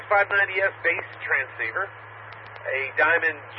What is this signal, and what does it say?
Emergency Alert System (SAME) tones, heard as harsh buzzing bursts, with speech. It is heard from 0.3 to 3.1 s.